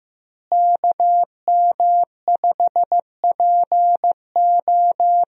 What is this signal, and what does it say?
Morse code, heard as a keyed tone, about 15 wpm, 700 Hz. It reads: KM5PO
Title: USA Call Signs - 5 Characters - Repeated 2x - 15wpm